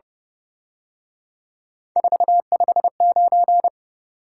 Morse code, heard as a keyed tone, about 27 wpm, 700 Hz. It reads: 459
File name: Speed-Racing - Three Letter-Number Combinations - 22wpm